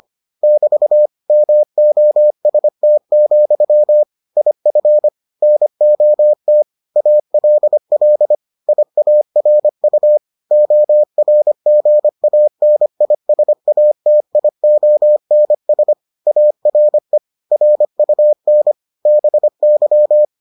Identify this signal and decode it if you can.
Morse code, heard as a keyed tone, about 25 wpm, 600 Hz. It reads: <BT> MOST, IF NOT ALL IARU ORGANISATIONS ARE RUN BY